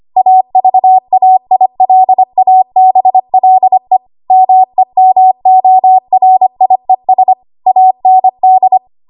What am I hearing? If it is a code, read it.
Morse code, heard as a keyed tone, about 25 wpm, 750 Hz. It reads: AVAILABLE MEMORIES AND
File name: Archive / 25 WPM / 25WPM